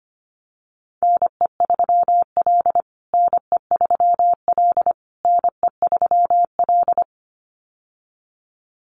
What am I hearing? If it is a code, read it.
Morse code, heard as a keyed tone, about 25 wpm, 700 Hz. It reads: NE3L NE3L NE3L
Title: USA Call Signs Repeated 3x 25wpm